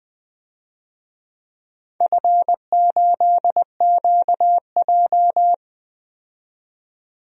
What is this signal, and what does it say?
Morse code, heard as a keyed tone, about 20 wpm, 700 Hz. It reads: F8QJ
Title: DXCC-100 Call Signs - 4 Characters - 20wpm